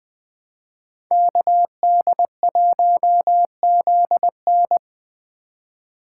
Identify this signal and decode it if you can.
Morse code, heard as a keyed tone, about 21 wpm, 700 Hz. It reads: KD1ZN